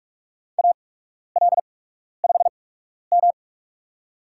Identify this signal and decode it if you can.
Morse code, heard as a keyed tone, about 45 wpm, spaced as for 12 wpm, 700 Hz. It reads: AL5M
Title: USA Call Signs - 4 Characters - 8x Character Spacing - 45wpm